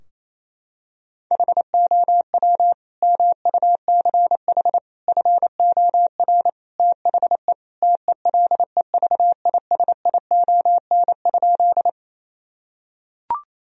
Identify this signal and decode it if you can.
Morse code, heard as a keyed tone, about 28 wpm, 700 Hz. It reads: HOW MUCH FOR THE TELEVISION?
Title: Sentences from Top 1000 Words - Review Part 4 - 28wpm